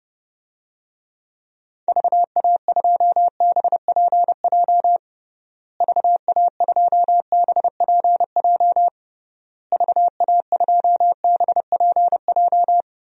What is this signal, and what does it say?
Morse code, heard as a keyed tone, about 32 wpm, 700 Hz. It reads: VA2BPJ VA2BPJ VA2BPJ
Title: DXCC-100 Call Signs - 6 Characters - Repeated 3x - 30wpm